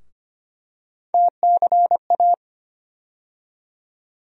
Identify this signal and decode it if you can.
Morse code, heard as a keyed tone, about 25 wpm, 700 Hz. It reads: TCA
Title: USA Call Sign Postfixes - 3 Letters 25wpm